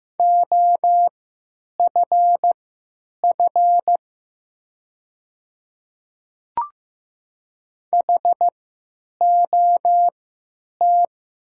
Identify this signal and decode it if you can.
Morse code, heard as a keyed tone, about 15 wpm, spaced as for 9 wpm, 700 Hz. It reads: OFF HOT
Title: 3x Character Spacing - Top 300 Words - 15wpm